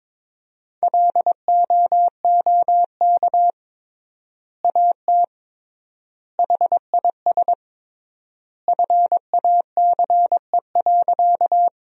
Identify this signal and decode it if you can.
Morse code, heard as a keyed tone, about 22 wpm, 700 Hz. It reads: LOOK AT HIS FACE.
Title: 3x Word-Spacing - Sentences from Top 300 Words - 22wpm